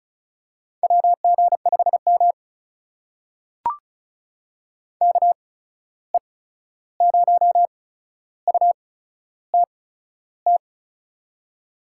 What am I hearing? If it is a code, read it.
Morse code, heard as a keyed tone, about 36 wpm, spaced as for 13 wpm, 700 Hz. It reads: WG5M KE0UTT
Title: USA Call Signs - 8x Character Spacing - 35wpm